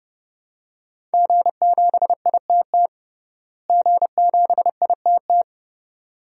Morse code, heard as a keyed tone, about 30 wpm, 700 Hz. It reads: G7ITT G7ITT
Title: DXCC-100 Call Signs - 5 Characters - Repeated 2x - 30wpm